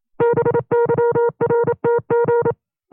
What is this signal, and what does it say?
Morse code, heard as a keyed tone, about 27 wpm, 470 Hz. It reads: BYRTG
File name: FreeFiles / CW-practice / 25WPM-side1